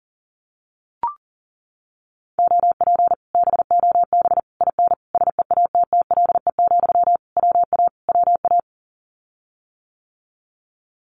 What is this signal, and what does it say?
Morse code, heard as a keyed tone, about 41 wpm, 700 Hz. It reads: OP BOB IN SEATTLE, WA WA
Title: Common QSO Phrases Encore 40wpm